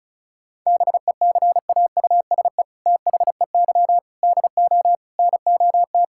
Morse code, heard as a keyed tone, about 36 wpm, 700 Hz. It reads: BECAUSE THEY DO NOT
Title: Sentences from Top 1000 Words - Review Part 5 - 35wpm